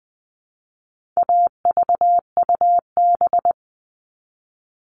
Morse code, heard as a keyed tone, about 20 wpm, 700 Hz. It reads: AVUB
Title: All Cut Numbers - 4-Digit 20wpm